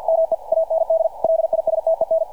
Morse code, heard as a keyed tone, about 49 wpm, 590 Hz, with radio noise. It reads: ME AND LISTEN
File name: archived / CRSnet